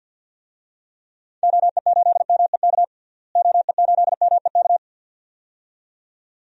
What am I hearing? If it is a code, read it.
Morse code, heard as a keyed tone, about 50 wpm, 700 Hz. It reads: OE8MEK OE8MEK
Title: DXCC-100 Call Signs - 6 Characters - Repeated 2x - 50wpm